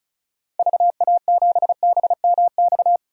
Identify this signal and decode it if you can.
Morse code, heard as a keyed tone, about 34 wpm, 700 Hz. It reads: VA7BMX
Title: DXCC-100 Call Signs - 6 Characters - Repeated 3x - 35wpm